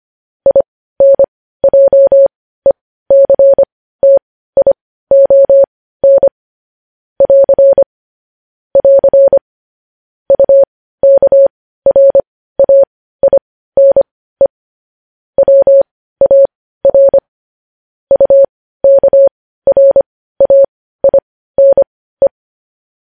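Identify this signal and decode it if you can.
Morse code, heard as a keyed tone, about 25 wpm, spaced as for 15 wpm, 550 Hz. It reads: INJECTION <AR> <AR> UKRAINE WAR UKRAINE
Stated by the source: Morse code news headlines sent at 15, preceded by vvv vvv.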